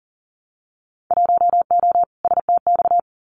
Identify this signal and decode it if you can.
Morse code, heard as a keyed tone, about 38 wpm, 700 Hz. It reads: 1O STX
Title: Winter Field Day Exchanges 40wpm